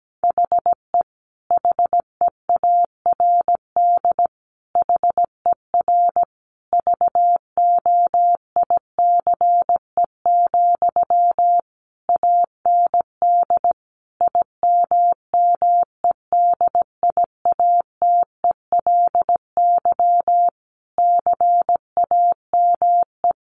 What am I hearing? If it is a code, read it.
Morse code, heard as a keyed tone, about 17 wpm, 700 Hz. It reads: HE HEARD HER VOICE, AND IMMEDIATELY CAME
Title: Sentences from Top 1100 Words 17wpm